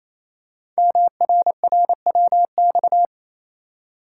Morse code, heard as a keyed tone, about 27 wpm, 700 Hz. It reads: MRRWX
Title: Code Groups - 5 Letters - 28wpm